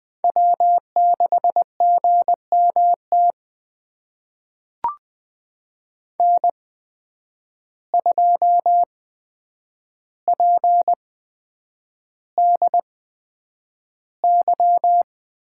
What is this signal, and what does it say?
Morse code, heard as a keyed tone, about 20 wpm, spaced as for 9 wpm, 700 Hz. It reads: W6GMT N2PDY
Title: USA Call Signs - 5 Characters - 8x Character Spacing - 20wpm